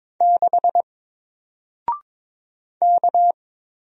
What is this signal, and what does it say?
Morse code, heard as a keyed tone, about 22 wpm, 700 Hz. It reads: 6 K
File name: Single-Letter-Number-22wpm